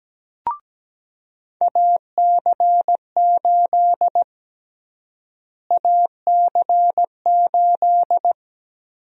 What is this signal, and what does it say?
Morse code, heard as a keyed tone, about 17 wpm, 700 Hz. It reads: AC8 AC8
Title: USA Call Sign Prefixes - Repeated 2x - 17wpm